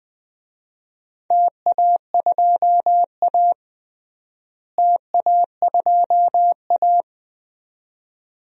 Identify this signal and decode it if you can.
Morse code, heard as a keyed tone, about 21 wpm, 700 Hz. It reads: TA2A TA2A
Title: DXCC-100 Call Signs - 4 Characters - Repeated 2x - 20wpm